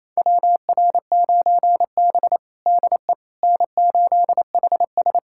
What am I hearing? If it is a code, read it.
Morse code, heard as a keyed tone, about 27 wpm, 700 Hz. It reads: WR9B DE N8HS
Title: USA Callsign DE Callsign - 4 Characters - 28wpm